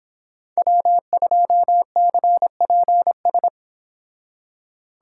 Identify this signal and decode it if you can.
Morse code, heard as a keyed tone, about 26 wpm, 700 Hz. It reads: W2CPS